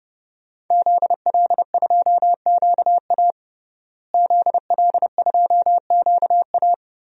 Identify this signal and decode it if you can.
Morse code, heard as a keyed tone, about 31 wpm, 700 Hz. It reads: ZL2QA ZL2QA
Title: DXCC-100 Call Signs - 5 Characters - Repeated 3x - 30wpm